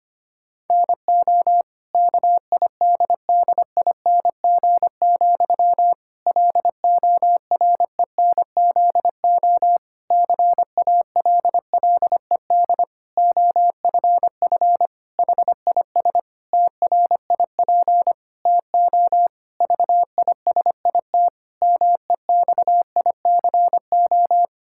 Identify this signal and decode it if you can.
Morse code, heard as a keyed tone, about 25 wpm, 700 Hz. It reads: NO KIDDING, LORENZO CALLED OFF HIS TRIP TO VISIT MEXICO
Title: Pangram Challenge v2 - 25wpm